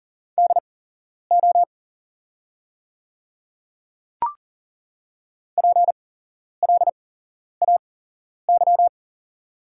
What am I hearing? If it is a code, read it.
Morse code, heard as a keyed tone, about 40 wpm, spaced as for 11 wpm, 700 Hz. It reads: DO PLAY
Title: 8x Character Spacing - Top 200 Words - 40wpm